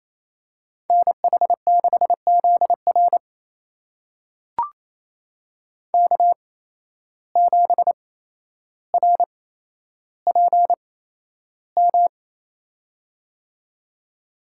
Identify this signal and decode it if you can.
Morse code, heard as a keyed tone, about 27 wpm, spaced as for 12 wpm, 700 Hz. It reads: NH6ZR K7RPM
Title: USA Call Signs - 8x Character Spacing - 28wpm